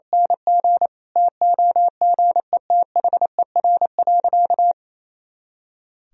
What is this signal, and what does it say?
Morse code, heard as a keyed tone, about 28 wpm, 700 Hz. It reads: NG TOGETHER.
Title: Sentences from Top 500 Words and Abbreviations 28wpm